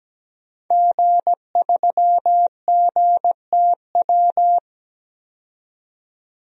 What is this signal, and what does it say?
Morse code, heard as a keyed tone, about 17 wpm, 700 Hz. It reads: G3GTW
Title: DXCC-100 Call Signs - 5 Characters - 17wpm